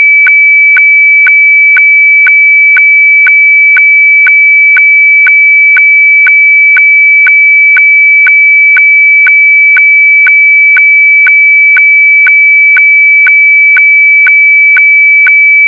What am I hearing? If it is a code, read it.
Weather fax (radiofax).